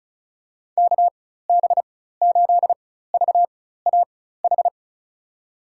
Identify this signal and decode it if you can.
Morse code, heard as a keyed tone, about 36 wpm, spaced as for 16 wpm, 700 Hz. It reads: KB8VAH
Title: USA Call Signs - 6 Characters - 4x Character Spacing - 35wpm